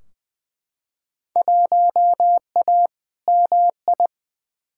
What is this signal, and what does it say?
Morse code, heard as a keyed tone, about 20 wpm, 700 Hz. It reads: 1A MI
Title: Field Day Exchanges - 20wpm